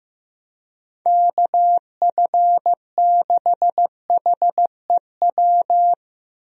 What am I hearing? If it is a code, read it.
Morse code, heard as a keyed tone, about 15 wpm, 700 Hz. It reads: KF6HEW
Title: USA Call Signs - 6 Characters - 15wpm